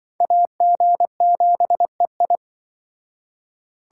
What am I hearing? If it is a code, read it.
Morse code, heard as a keyed tone, about 25 wpm, 700 Hz. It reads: AG7EI
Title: Speed Racing - USA 2x2 Call Signs 20wpm